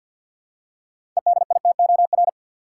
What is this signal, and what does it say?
Morse code, heard as a keyed tone, about 51 wpm, 700 Hz. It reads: EDITOR
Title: Six Letter Words - 50wpm